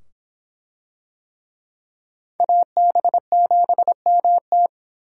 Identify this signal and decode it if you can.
Morse code, heard as a keyed tone, about 25 wpm, 700 Hz. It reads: AB7MT